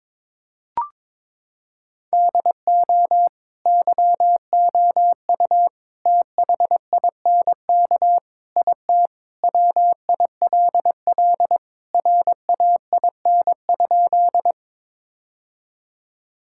Morse code, heard as a keyed tone, about 22 wpm, 700 Hz. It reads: DO YOU THINK IT WILL RAIN?
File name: practice / 22wpm / Sentences with 4-Word Phrases - 22wpm